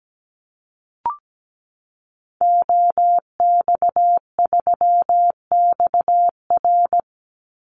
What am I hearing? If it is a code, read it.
Morse code, heard as a keyed tone, about 17 wpm, 700 Hz. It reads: OX3XR
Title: Call Signs Repeated 3x - 17wpm